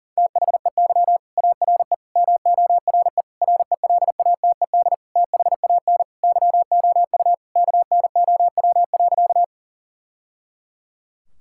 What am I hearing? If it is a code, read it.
Morse code, heard as a keyed tone, about 40 wpm, 700 Hz. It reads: THEY ARE MORE RELATED THAN YOU KNOW.
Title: Sentences from Top 1000 Words - Review Part 5 - 40wpm